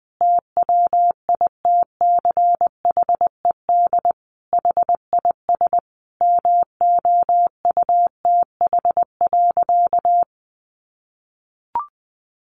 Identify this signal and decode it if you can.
Morse code, heard as a keyed tone, about 20 wpm, 700 Hz. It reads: TWITCHED HIS MOUTH.
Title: A Sound Of Thunder 20wpm